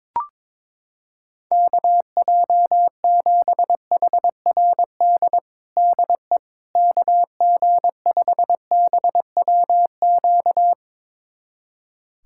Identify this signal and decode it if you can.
Morse code, heard as a keyed tone, about 22 wpm, 700 Hz. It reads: KJ7HRD DE KG5BWQ
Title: USA Callsign DE Callsign 22wpm